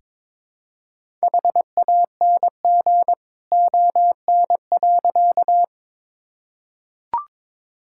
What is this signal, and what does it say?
Morse code, heard as a keyed tone, about 22 wpm, 700 Hz. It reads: HANG ON.
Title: Sentences from Top 1600 Words 22wpm